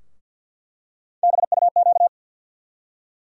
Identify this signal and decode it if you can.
Morse code, heard as a keyed tone, about 48 wpm, 700 Hz. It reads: BUQ
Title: USA Call Sign Postfixes - 3 Letters 50wpm